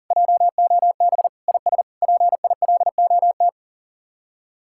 Morse code, heard as a keyed tone, about 54 wpm, spaced as for 47 wpm, 700 Hz. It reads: JOB IS PILOT
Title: Speed-Racing Common QSO Phrases - 40wpm